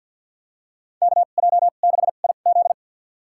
Morse code, heard as a keyed tone, about 52 wpm, spaced as for 37 wpm, 700 Hz. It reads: KJ6IZ
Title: USA Call Signs - 2x Character Spacing - 50wpm